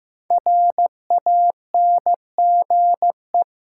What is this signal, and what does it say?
Morse code, heard as a keyed tone, about 15 wpm, 700 Hz. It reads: RANGE
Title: Sets of 4 Words - New Word Drill - 15wpm